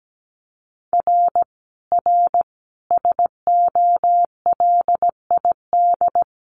Morse code, heard as a keyed tone, about 17 wpm, 700 Hz. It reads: R R SOLID
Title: Common QSO Phrases 17wpm